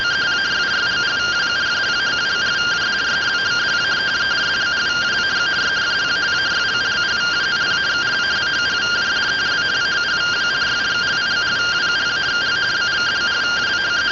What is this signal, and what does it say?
Two-tone teleprinter signal, about 170 Hz shift.